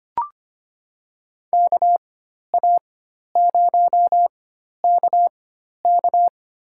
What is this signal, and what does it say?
Morse code, heard as a keyed tone, about 25 wpm, spaced as for 12 wpm, 700 Hz. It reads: KA0KK
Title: USA Call Signs - 4x Character Spacing - 25wpm